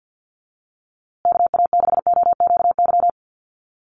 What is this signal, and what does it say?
Morse code, heard as a keyed tone, about 50 wpm, 700 Hz. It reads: KA6OQY
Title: USA Call Signs Repeated 2x 50wpm